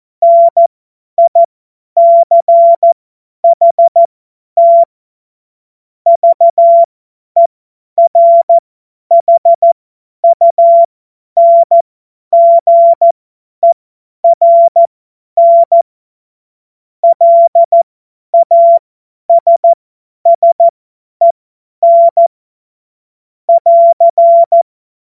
Morse code, heard as a keyed tone, about 14 wpm, spaced as for 10 wpm, 680 Hz. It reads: NICHT VERHUNGERN LASSEN <AR>
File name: Witz-08-Juli-2025-50-F70